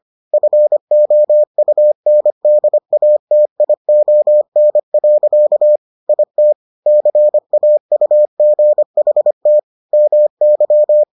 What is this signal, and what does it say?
Morse code, heard as a keyed tone, about 25 wpm, 600 Hz. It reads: FOUNDATION. IT CAUGHT MY